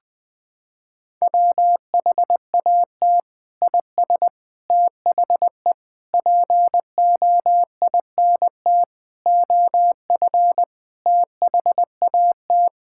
Morse code, heard as a keyed tone, about 19 wpm, 700 Hz. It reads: WHAT IS THE POINT OF THAT
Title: Sentences from Top 1000 Words - Review Part 5 - 20wpm